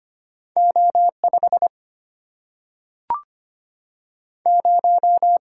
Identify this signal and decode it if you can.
Morse code, heard as a keyed tone, about 25 wpm, 700 Hz. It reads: O 5 0
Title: Single-Letter Single-Number 25wpm